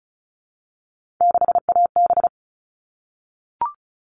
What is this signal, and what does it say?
Morse code, heard as a keyed tone, about 34 wpm, 700 Hz. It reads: 6AB
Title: All Cut Numbers - 3-Digit 35wpm